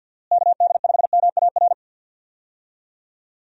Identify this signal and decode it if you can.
Morse code, heard as a keyed tone, about 50 wpm, 700 Hz. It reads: KD5MAR
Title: USA Call Signs - 6 Characters - Repeated 3x - 50wpm